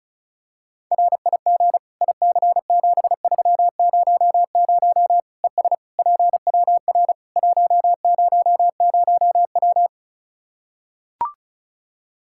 Morse code, heard as a keyed tone, about 34 wpm, 700 Hz. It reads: RIG IC7300 ES PWR 100W
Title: Practice QSO - Exchange 2 - Info - 35wpm